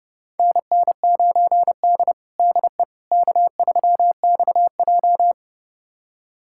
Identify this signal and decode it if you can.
Morse code, heard as a keyed tone, about 30 wpm, 700 Hz. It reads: NN9D DE K3XJ
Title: USA Callsign DE Callsign - 4 Characters - 30wpm